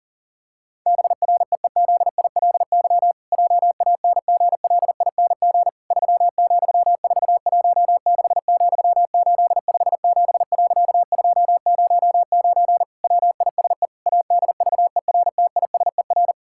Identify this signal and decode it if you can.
Morse code, heard as a keyed tone, about 37 wpm, 700 Hz. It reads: BREEZILY JANGLING 3,416,857.209 WISE ADVERTISER
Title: Pangram Challenge v2 - 40wpm